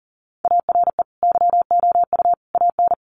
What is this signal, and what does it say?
Morse code, heard as a keyed tone, about 40 wpm, 700 Hz. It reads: ARE YOU AN
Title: Sentences from Top 1200 Words 40wpm